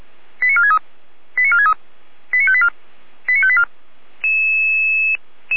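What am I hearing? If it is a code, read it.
Five-tone selective call (ZVEI), heard as quick runs of five tones, tones of about 70 ms. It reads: 89463 89463 89584 89584
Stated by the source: ZVEI five-tone selective call received by SDR.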